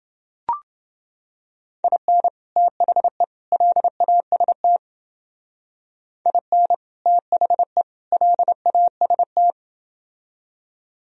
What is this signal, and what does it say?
Morse code, heard as a keyed tone, about 29 wpm, 700 Hz. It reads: IN THE LAST IN THE LAST